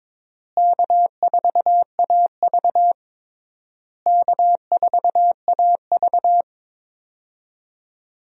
Morse code, heard as a keyed tone, about 23 wpm, 700 Hz. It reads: K4AV K4AV
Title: USA Call Signs - 4 Characters - Repeated 3x - 22wpm